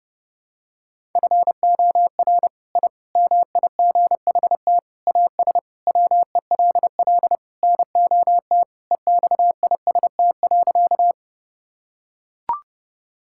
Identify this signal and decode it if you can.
Morse code, heard as a keyed tone, about 30 wpm, 700 Hz. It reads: FOR I MIGHT AS WELL NOT EXIST.
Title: Sentences from Top 1000 Words - Review Part 5 - 30wpm